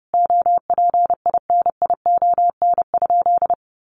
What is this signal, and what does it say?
Morse code, heard as a keyed tone, about 30 wpm, 700 Hz. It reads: OPINION?
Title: Sentences from Top 1200 Words 30wpm